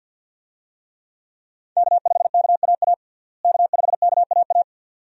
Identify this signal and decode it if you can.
Morse code, heard as a keyed tone, about 46 wpm, 700 Hz. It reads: K5KAA K5KAA
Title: USA Call Signs - 5 Characters - Repeated 2x - 50wpm